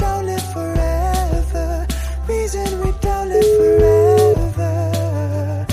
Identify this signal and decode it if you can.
Ringback tone (Europe, 425 Hz), with noise.